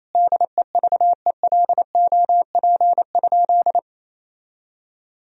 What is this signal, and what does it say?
Morse code, heard as a keyed tone, about 28 wpm, 700 Hz. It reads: DEVELOP?
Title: Sentences from Top 1000 Words - Review Part 2 - 28wpm